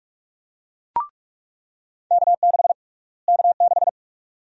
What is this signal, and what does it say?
Morse code, heard as a keyed tone, about 44 wpm, 700 Hz. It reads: K6 K6
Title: USA Call Sign Prefixes - Repeated 2x - 45wpm